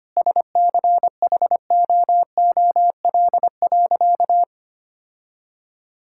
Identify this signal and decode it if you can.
Morse code, heard as a keyed tone, about 25 wpm, 700 Hz. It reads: SCHOOL.
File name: practice / 25wpm / Sentences with 4-Word Phrases - 25wpm